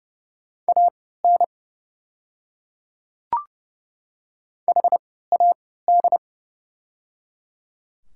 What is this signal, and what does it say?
Morse code, heard as a keyed tone, about 30 wpm, spaced as for 17 wpm, 700 Hz. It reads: AN HAD
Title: 3x Character Spacing - Top 100 Words - 30wpm